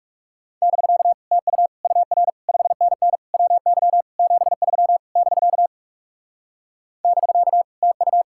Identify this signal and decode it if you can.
Morse code, heard as a keyed tone, about 40 wpm, 700 Hz. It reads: <BK> TU UR 5NN WY 73 <BK> <BK> TU